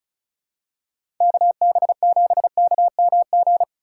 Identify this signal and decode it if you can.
Morse code, heard as a keyed tone, about 34 wpm, 700 Hz. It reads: KB7KMG
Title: USA Call Signs - 6 Characters - 3x Character Spacing - 35wpm